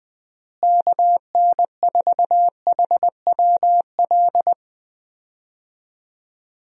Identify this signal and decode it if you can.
Morse code, heard as a keyed tone, about 21 wpm, 700 Hz. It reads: KN4HWL
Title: USA Call Signs - 6 Characters - Repeated 3x - 20wpm